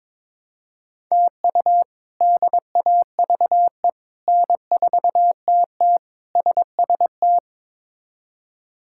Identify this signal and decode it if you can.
Morse code, heard as a keyed tone, about 22 wpm, 700 Hz. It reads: TU DAVE N4TT SST
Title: K1USN SST QSO Practice - 22wpm